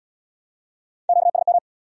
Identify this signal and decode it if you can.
Morse code, heard as a keyed tone, about 75 wpm, 700 Hz. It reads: QSM